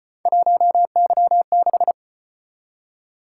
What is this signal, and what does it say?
Morse code, heard as a keyed tone, about 33 wpm, 700 Hz. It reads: 1Y6